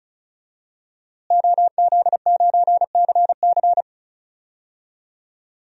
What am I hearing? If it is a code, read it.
Morse code, heard as a keyed tone, about 36 wpm, 700 Hz. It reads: OZ9CC
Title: DXCC-100 Call Signs - 5 Characters - Repeated 2x - 35wpm